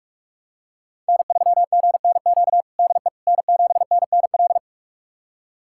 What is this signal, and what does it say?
Morse code, heard as a keyed tone, about 42 wpm, 700 Hz. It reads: N3GNQ DE N7NNL
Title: USA Callsign DE Callsign - 5 Characters - 45wpm